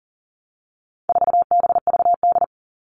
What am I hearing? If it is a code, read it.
Morse code, heard as a keyed tone, about 40 wpm, 700 Hz. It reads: 4BVD
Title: All Cut Numbers - 4-Digit 40wpm